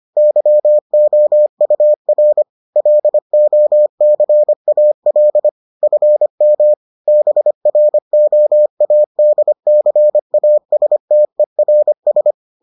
Morse code, heard as a keyed tone, about 25 wpm, 600 Hz. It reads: YOUR LOCAL FM BROADCASTERS